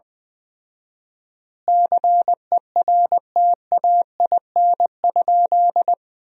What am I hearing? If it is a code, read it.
Morse code, heard as a keyed tone, about 20 wpm, 700 Hz. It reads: CERTAIN?
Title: 4x Word-Spacing - Sentences from Top 500 Words - 20wpm